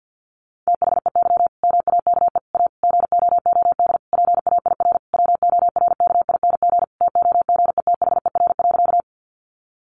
Morse code, heard as a keyed tone, about 50 wpm, 700 Hz. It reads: THEY MAKE A GOOD PAIR WORKING TOGETHER.
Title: Sentences from Top 1200 Words 50wpm